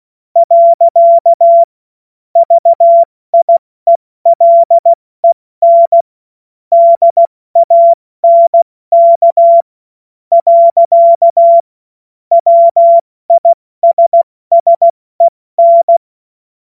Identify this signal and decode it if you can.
Morse code, heard as a keyed tone, about 16 wpm, spaced as for 14 wpm, 680 Hz. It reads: . VIELEN DANK. WISSEN
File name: Morsekurs / Witze / Witz 06 Februar 2024 70 F80